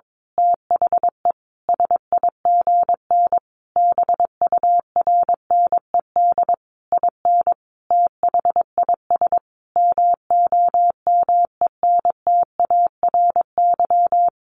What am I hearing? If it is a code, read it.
Morse code, heard as a keyed tone, about 22 wpm, 700 Hz. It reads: THE SIGN BURNED IN THIS MOMENTARY
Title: A Sound Of Thunder 22wpm